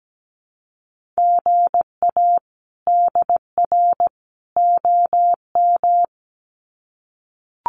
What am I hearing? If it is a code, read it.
Morse code, heard as a keyed tone, about 17 wpm, 700 Hz. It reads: GA DR OM
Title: Common QSO Phrases 17wpm